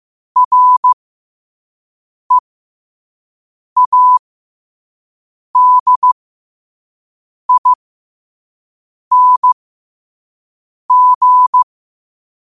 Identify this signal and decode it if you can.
Morse code, heard as a keyed tone, about 15 wpm, spaced as for 5 wpm, 1000 Hz. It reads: READING
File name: qotdfiles / 5wpm